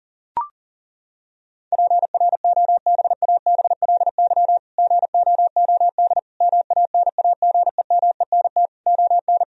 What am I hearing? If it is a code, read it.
Morse code, heard as a keyed tone, about 40 wpm, 700 Hz. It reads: PROBABLY GOOD MANAGEMENT ON
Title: Sentences from Top 1000 Words - Review Part 4 - 40wpm